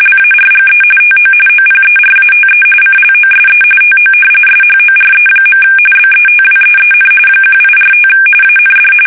RTTY, about 45.45 baud. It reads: LARGE TO OPERATE CLOSE TO SHORE. BETWEEN 1845 AND
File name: fsk850 / usnavy2